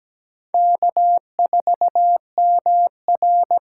Morse code, heard as a keyed tone, about 17 wpm, 700 Hz. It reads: K4MR
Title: USA Call Signs - 8x Character Spacing - 17wpm